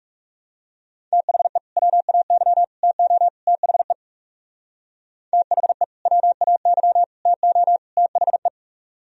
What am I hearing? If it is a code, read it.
Morse code, heard as a keyed tone, about 37 wpm, 700 Hz. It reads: THE WAY TO THE THE WAY TO THE
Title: Speed-Racing - Sets of 4 Words 30wpm